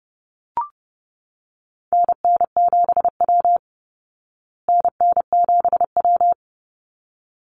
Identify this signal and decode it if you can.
Morse code, heard as a keyed tone, about 30 wpm, 700 Hz. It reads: NN7W NN7W
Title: Call Signs Repeated 3x - 30wpm